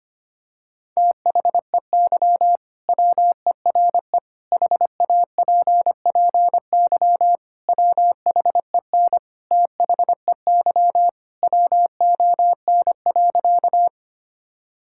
Morse code, heard as a keyed tone, about 25 wpm, 690 Hz. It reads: THEY WERE HAPPY WHEN THEY WON.
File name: practice / 25wpm / Sentences with 2-Word Phrases - 25wpm